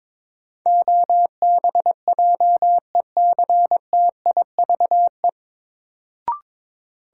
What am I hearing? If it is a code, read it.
Morse code, heard as a keyed tone, about 22 wpm, 700 Hz. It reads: OBJECTIVE
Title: Sentences from Top 1200 Words - New Word Drill - 22wpm